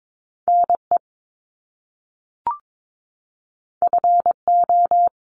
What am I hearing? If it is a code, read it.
Morse code, heard as a keyed tone, about 22 wpm, 700 Hz. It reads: NE FO
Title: Top Two Letter Combinations in Words 22wpm